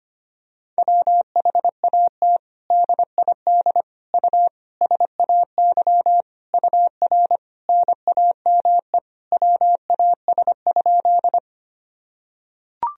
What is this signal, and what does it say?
Morse code, heard as a keyed tone, about 25 wpm, 700 Hz. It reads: WHAT DID U SAY UR NAME WAS?
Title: Sentences from Top 200 Words and Abbreviations 25wpm